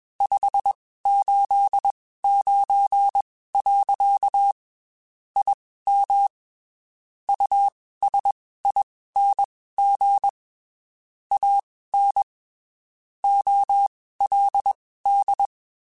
Morse code, heard as a keyed tone, about 21 wpm, spaced as for 15 wpm, 800 Hz. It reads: 589. IM USING AN OLD
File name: QSOs / QSOs-mp3s / WD0FBT 1@20